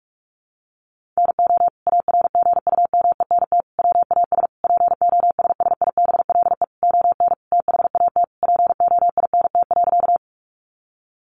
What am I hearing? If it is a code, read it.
Morse code, heard as a keyed tone, about 44 wpm, 700 Hz. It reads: NO ARGUMENT WAS POSSIBLE ON THAT POINT.
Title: Sentences from Top 1400 Words 45wpm